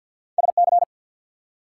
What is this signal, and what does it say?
Morse code, heard as a keyed tone, about 50 wpm, 700 Hz. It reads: SC
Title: US States and Canadian Provinces 50wpm